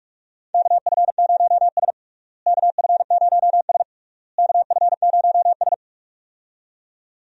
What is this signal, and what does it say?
Morse code, heard as a keyed tone, about 41 wpm, 700 Hz. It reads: KF0S KF0S KF0S
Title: USA Call Signs - 4 Characters - Repeated 3x - 45wpm